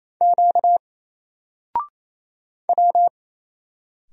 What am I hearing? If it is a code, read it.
Morse code, heard as a keyed tone, about 27 wpm, 700 Hz. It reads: Q W
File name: Single-Letter-Number-28wpm